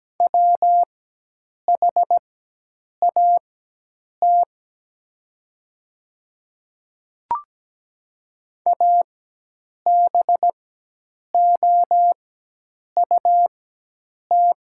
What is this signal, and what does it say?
Morse code, heard as a keyed tone, about 17 wpm, spaced as for 8 wpm, 700 Hz. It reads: WHAT ABOUT
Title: 4x Character Spacing - Top 100 Words - 17wpm